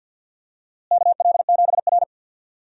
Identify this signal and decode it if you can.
Morse code, heard as a keyed tone, about 52 wpm, 700 Hz. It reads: KL7R
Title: DXCC-100 Call Signs - 4 Characters - Repeated 2x - 50wpm